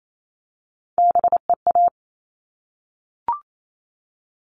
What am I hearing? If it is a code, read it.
Morse code, heard as a keyed tone, about 28 wpm, 700 Hz. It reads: BEA
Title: All Cut Numbers - 3-Digit 28wpm